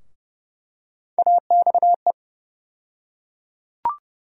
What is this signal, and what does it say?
Morse code, heard as a keyed tone, about 31 wpm, 700 Hz. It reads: AXE
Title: USA Call Sign Postfixes - 3 Letters 30wpm